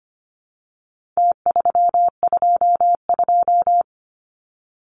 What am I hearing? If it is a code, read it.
Morse code, heard as a keyed tone, about 25 wpm, 700 Hz. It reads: T322
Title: Numbers - Contest Style - 4-Digit 25wpm